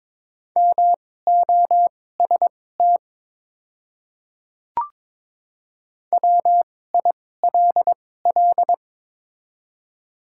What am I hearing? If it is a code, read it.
Morse code, heard as a keyed tone, about 22 wpm, spaced as for 16 wpm, 700 Hz. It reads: MOST WILL
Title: 2x Character Spacing - Top 100 Words - 22wpm